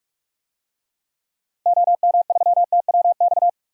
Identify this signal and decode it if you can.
Morse code, heard as a keyed tone, about 44 wpm, 700 Hz. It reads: OM3TWX
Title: DXCC-100 Call Signs - 6 Characters - 45wpm